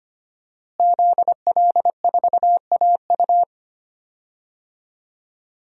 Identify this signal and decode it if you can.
Morse code, heard as a keyed tone, about 25 wpm, 700 Hz. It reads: ZL4AU
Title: DXCC-100 Call Signs - 5 Characters - Repeated 2x - 25wpm